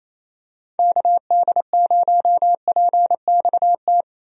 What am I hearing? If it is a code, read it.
Morse code, heard as a keyed tone, about 28 wpm, 700 Hz. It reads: KD0PXT
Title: USA Call Signs - 6 Characters - Repeated 2x - 28wpm